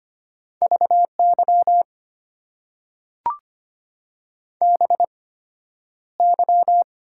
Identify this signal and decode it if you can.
Morse code, heard as a keyed tone, about 25 wpm, spaced as for 11 wpm, 690 Hz. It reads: VY BY
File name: Top 90 QSO Elements - 8x Character Spacing - 25wpm